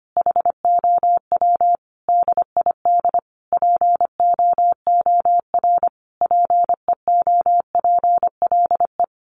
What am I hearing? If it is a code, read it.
Morse code, heard as a keyed tone, about 25 wpm, 700 Hz. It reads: HOW DID POOR PEOPLE
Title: Sentences from Top 1100 Words 25wpm